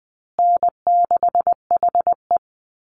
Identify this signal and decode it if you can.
Morse code, heard as a keyed tone, about 20 wpm, 700 Hz. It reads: N6HE
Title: Call Signs Repeated 3x - 20wpm